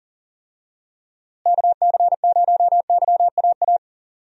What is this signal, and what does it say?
Morse code, heard as a keyed tone, about 39 wpm, 700 Hz. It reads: KC0YAA
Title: USA Call Signs - 6 Characters - Repeated 3x - 40wpm